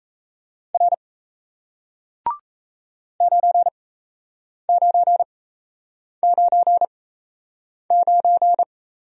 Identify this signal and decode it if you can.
Morse code, heard as a keyed tone, about 37 wpm, spaced as for 8 wpm, 700 Hz. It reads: R 9999